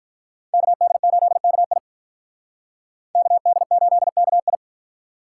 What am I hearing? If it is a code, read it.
Morse code, heard as a keyed tone, about 46 wpm, 700 Hz. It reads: KD8KI KD8KI